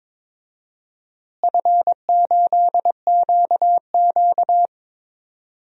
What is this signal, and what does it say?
Morse code, heard as a keyed tone, about 22 wpm, 700 Hz. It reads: F8QQ
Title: DXCC-100 Call Signs - 4 Characters - Repeated 3x - 22wpm